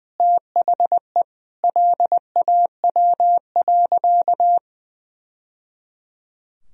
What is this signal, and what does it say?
Morse code, heard as a keyed tone, about 21 wpm, 700 Hz. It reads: THE LAW.
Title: Sentences from Top 1000 Words - Review Part 3 - 20wpm